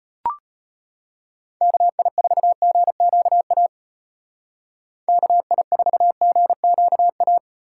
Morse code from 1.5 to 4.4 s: KI4GQA

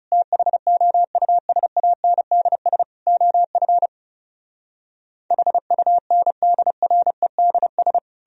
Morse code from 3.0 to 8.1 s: OF HUNDREDS

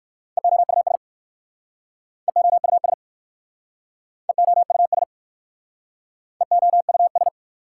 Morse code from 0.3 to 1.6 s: EOUS

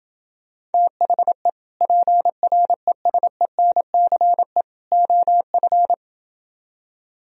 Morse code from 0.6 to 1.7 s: THE

